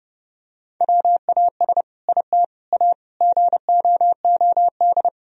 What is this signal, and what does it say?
Morse code, heard as a keyed tone, about 30 wpm, 700 Hz. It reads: WAS IT A GOOD
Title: Sentences from Top 1000 Words - Review Part 4 - 30wpm